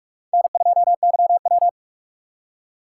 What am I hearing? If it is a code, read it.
Morse code, heard as a keyed tone, about 44 wpm, 700 Hz. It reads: N2YW
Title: USA Call Signs - 6x Character Spacing - 45wpm